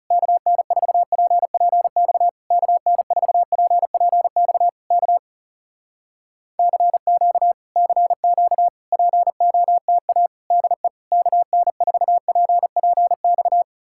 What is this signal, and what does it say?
Morse code, heard as a keyed tone, about 40 wpm, 700 Hz. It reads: KN4PPX KN4PPX K CQ CQ POTA DE KN4PPX